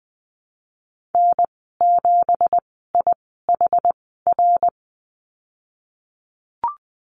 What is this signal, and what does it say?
Morse code, heard as a keyed tone, about 20 wpm, spaced as for 15 wpm, 700 Hz. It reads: N7IHR
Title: 173 Lesson 28g - Number 9 - Call Signs 20wpm